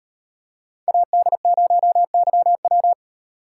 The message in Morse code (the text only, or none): AD0YW